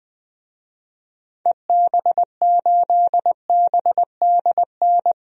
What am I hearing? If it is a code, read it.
Morse code, heard as a keyed tone, about 20 wpm, 700 Hz. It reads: EB8BDN
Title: DXCC-100 Call Signs - 6 Characters - 20wpm